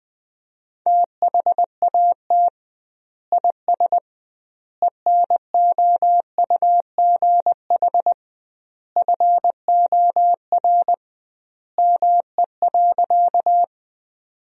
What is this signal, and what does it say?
Morse code, heard as a keyed tone, about 20 wpm, 700 Hz. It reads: THAT IS ENOUGH FOR ME.
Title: 2x Word-Spacing - Sentences from Top 400 Words - 20wpm